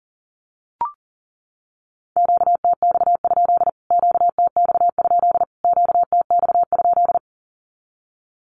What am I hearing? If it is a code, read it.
Morse code, heard as a keyed tone, about 40 wpm, 700 Hz. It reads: QTX? QTX? QTX?
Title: Q-Codes Repeated 3x 40wpm